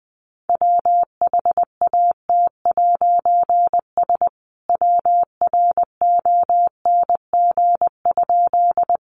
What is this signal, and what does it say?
Morse code, heard as a keyed tone, about 20 wpm, 700 Hz. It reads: WHAT'S WRONG?
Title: A Sound Of Thunder 20wpm